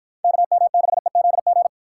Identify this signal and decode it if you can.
Morse code, heard as a keyed tone, about 52 wpm, 700 Hz. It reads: KM6EZG